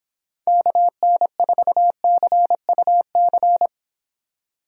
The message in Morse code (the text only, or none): KN4CUC